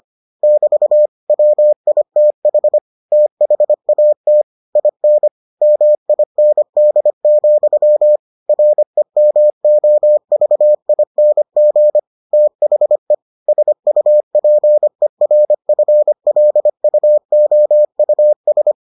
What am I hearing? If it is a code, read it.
Morse code, heard as a keyed tone, about 25 wpm, 600 Hz. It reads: <BT> WITH THAT IN MIND, REMOVING THE SUPERFLUOUS